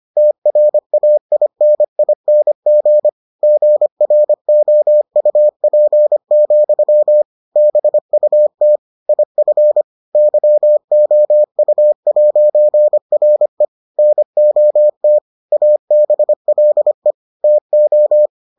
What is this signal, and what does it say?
Morse code, heard as a keyed tone, about 25 wpm, 600 Hz. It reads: TRAINING GROUP, BUT IF YOU'RE NOT ABLE TO